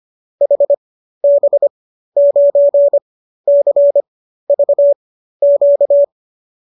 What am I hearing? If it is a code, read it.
Morse code, heard as a keyed tone, about 25 wpm, spaced as for 13 wpm, 570 Hz. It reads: HB9CVQ